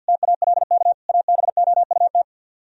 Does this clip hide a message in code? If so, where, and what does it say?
Yes: Morse code from 0.1 to 2.2 s, heard as a keyed tone, about 64 wpm, 700 Hz. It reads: TALK ABOUT